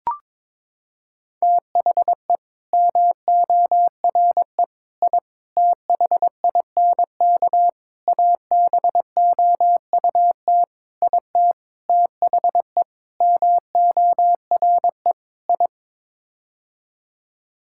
Morse code, heard as a keyed tone, about 22 wpm, 700 Hz. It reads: THE MORE I THINK ABOUT IT THE MORE I
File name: Sets of 9 Words - 22wpm